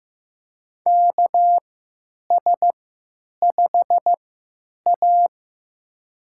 Morse code, heard as a keyed tone, about 15 wpm, spaced as for 9 wpm, 700 Hz. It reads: KS5A
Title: USA Call Signs - 4 Characters - 3x Character Spacing - 15wpm